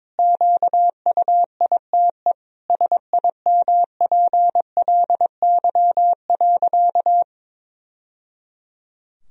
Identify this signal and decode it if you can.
Morse code, heard as a keyed tone, about 22 wpm, 700 Hz. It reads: QUITE SIMPLY.
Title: Sentences from Top 1000 Words - Review Part 4 - 22wpm